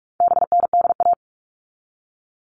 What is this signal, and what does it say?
Morse code, heard as a keyed tone, about 44 wpm, 700 Hz. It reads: BNDA
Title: All Cut Numbers - 4-Digit 45wpm